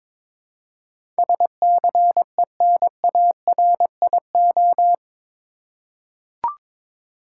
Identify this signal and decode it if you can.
Morse code, heard as a keyed tone, about 22 wpm, 700 Hz. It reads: SCENARIO
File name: practice / 22wpm / Eight Letter Words - 22wpm